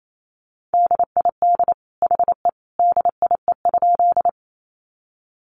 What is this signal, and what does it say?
Morse code, heard as a keyed tone, about 28 wpm, 700 Hz. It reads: DID HE DIE?
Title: Sentences from Top 1300 Words 28wpm